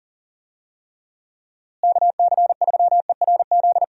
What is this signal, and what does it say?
Morse code, heard as a keyed tone, about 40 wpm, 700 Hz. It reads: KC3ERZ
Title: USA Call Signs - 6 Characters - Repeated 3x - 40wpm